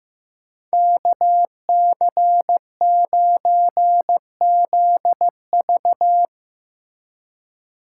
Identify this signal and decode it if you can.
Morse code, heard as a keyed tone, about 16 wpm, 700 Hz. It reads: KC9ZV
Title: Speed Racing - USA 2x2 Call Signs 15wpm